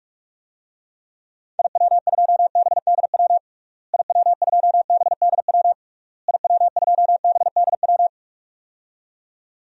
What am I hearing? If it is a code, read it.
Morse code, heard as a keyed tone, about 41 wpm, 700 Hz. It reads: IW2BDW IW2BDW IW2BDW
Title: DXCC-100 Call Signs - 6 Characters - Repeated 3x - 45wpm